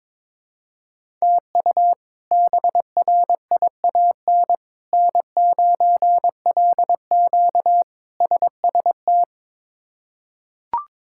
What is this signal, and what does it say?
Morse code, heard as a keyed tone, about 22 wpm, 700 Hz. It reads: TU BRIAN N9LQ SST
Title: K1USN SST QSO Practice - 22wpm